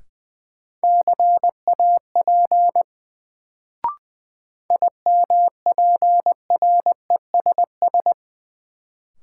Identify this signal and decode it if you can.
Morse code, heard as a keyed tone, about 20 wpm, 700 Hz. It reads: CAP IMPRESS
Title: Sentences from Top 1600 Words - New Word Drill - 20wpm